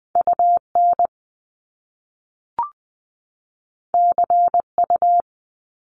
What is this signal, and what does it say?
Morse code, heard as a keyed tone, about 20 wpm, 700 Hz. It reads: UN CU
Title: Top Two Letter Combinations in Words 20wpm